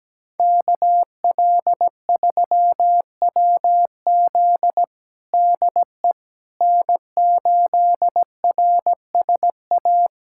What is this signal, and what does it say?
Morse code, heard as a keyed tone, about 17 wpm, 700 Hz. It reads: KL3WZ DE N8RSA
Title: USA Callsign DE Callsign - 5 Characters - 17wpm